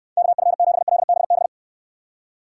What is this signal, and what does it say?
Morse code, heard as a keyed tone, about 63 wpm, 700 Hz. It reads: KF7LBB